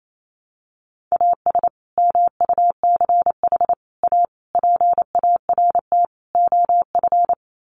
Morse code, heard as a keyed tone, about 28 wpm, 700 Hz. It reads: AS MUCH A PART OF
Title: Sets of 5 Words Encore2 28wpm